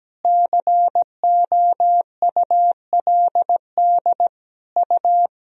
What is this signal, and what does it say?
Morse code, heard as a keyed tone, about 17 wpm, 700 Hz. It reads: COULD U
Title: Sentences from Top 300 Words and Abbreviations 17wpm